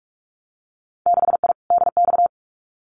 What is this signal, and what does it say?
Morse code, heard as a keyed tone, about 46 wpm, 700 Hz. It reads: 6I DX
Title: Winter Field Day Exchanges Encore 45wpm